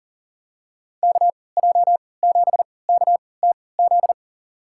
Morse code, heard as a keyed tone, about 40 wpm, spaced as for 23 wpm, 700 Hz. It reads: KJ7KTZ
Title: USA Call Signs - 6 Characters - 3x Character Spacing - 40wpm